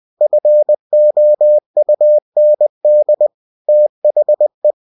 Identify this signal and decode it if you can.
Morse code, heard as a keyed tone, about 20 wpm, 600 Hz. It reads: FOUND THE